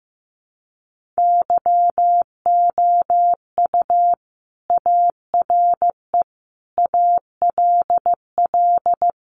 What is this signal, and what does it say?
Morse code, heard as a keyed tone, about 15 wpm, 700 Hz. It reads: YOU ARE ALL
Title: Sentences from Top 900 Words 15wpm